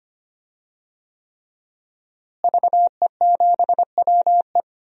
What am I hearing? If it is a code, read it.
Morse code, heard as a keyed tone, about 25 wpm, 700 Hz. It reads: VE7WE
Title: DXCC-100 Call Signs - 5 Characters - Repeated 3x - 25wpm